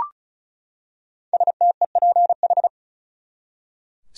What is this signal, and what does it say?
Morse code, heard as a keyed tone, about 35 wpm, 700 Hz. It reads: STEPH
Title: USA Common Names - Max 5 Letters - 35wpm